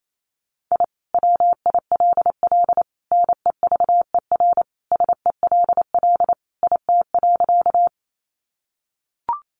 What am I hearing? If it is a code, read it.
Morse code, heard as a keyed tone, about 28 wpm, 700 Hz. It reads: I WILL NEVER SELL IT.
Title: Sentences from Top 1100 Words 28wpm